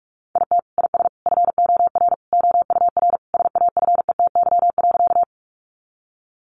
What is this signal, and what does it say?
Morse code, heard as a keyed tone, about 44 wpm, 700 Hz. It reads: IT IS FOR OUR SAFETY.
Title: Sentences from Top 700 Words 45wpm